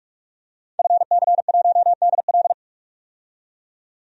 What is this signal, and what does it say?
Morse code, heard as a keyed tone, about 46 wpm, 700 Hz. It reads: FC1DL
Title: DXCC-100 Call Signs - 5 Characters - 45wpm